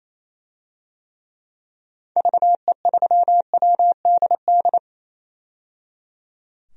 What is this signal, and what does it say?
Morse code, heard as a keyed tone, about 28 wpm, 700 Hz. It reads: VE3WDD
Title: DXCC-100 Call Signs - 6 Characters - 28wpm